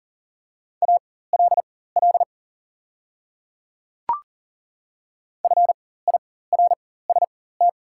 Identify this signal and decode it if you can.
Morse code, heard as a keyed tone, about 40 wpm, spaced as for 19 wpm, 690 Hz. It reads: ALL FIRST
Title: 4x Character Spacing - Top 300 Words - 40wpm